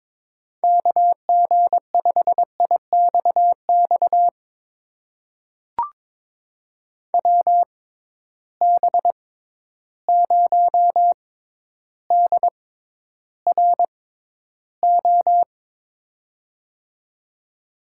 Morse code, heard as a keyed tone, about 21 wpm, spaced as for 11 wpm, 700 Hz. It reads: KG5IXX WB0DRO
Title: USA Call Signs - 6 Characters - 6x Character Spacing - 22wpm